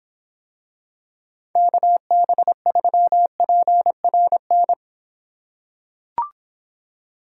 Morse code, heard as a keyed tone, about 26 wpm, 700 Hz. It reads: KB3PRN